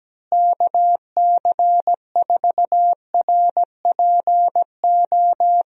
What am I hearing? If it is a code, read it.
Morse code, heard as a keyed tone, about 17 wpm, 700 Hz. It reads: KC4RPO
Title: USA Call Signs - 6 Characters - Repeated 3x - 17wpm